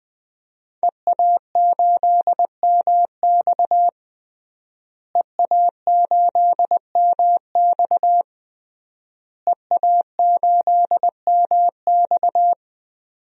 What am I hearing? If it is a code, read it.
Morse code, heard as a keyed tone, about 20 wpm, 700 Hz. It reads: EA8MX EA8MX EA8MX
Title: DXCC-100 Call Signs - 5 Characters - Repeated 3x - 20wpm